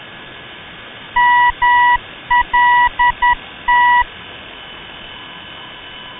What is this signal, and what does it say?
Morse code, heard as a keyed tone, about 10 wpm, 970 Hz. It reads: MLT
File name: ndb malta mlt-395